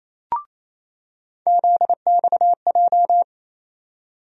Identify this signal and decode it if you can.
Morse code, heard as a keyed tone, about 28 wpm, 700 Hz. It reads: ZXJ
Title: USA Call Sign Postfixes - 3 Letters 28wpm